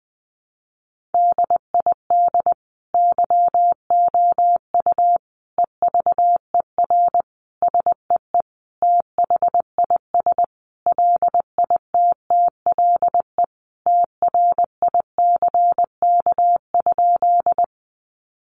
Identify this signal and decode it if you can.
Morse code, heard as a keyed tone, about 20 wpm, 700 Hz. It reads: DID YOU EVER SEE THIS LITTLE TRICK?
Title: Sentences from Top 1500 Words 20wpm